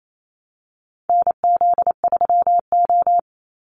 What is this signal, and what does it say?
Morse code, heard as a keyed tone, about 28 wpm, 700 Hz. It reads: NZ3O
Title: Call Signs Repeated 3x - 28wpm